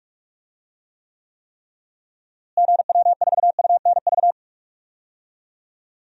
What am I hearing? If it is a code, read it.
Morse code, heard as a keyed tone, about 48 wpm, 700 Hz. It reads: GW4UNV
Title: DXCC-100 Call Signs - 6 Characters - Repeated 3x - 45wpm